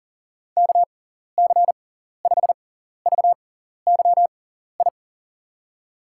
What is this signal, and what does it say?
Morse code, heard as a keyed tone, about 40 wpm, spaced as for 14 wpm, 700 Hz. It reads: KC5VYI
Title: USA Call Signs - 6 Characters - 6x Character Spacing - 40wpm